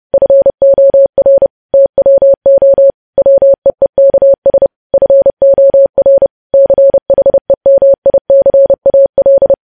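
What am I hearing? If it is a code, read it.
Morse code, heard as a keyed tone, about 30 wpm, 550 Hz. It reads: FOR TWO WEEKS FOR CHEMICAL
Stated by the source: Morse code news headlines practice, preceded by vvv.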